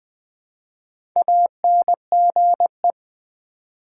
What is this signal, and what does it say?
Morse code, heard as a keyed tone, about 20 wpm, 700 Hz. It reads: ANGE
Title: Build A Word - 1 Letter At A Time - 3 to 5-Letter Words - 20wpm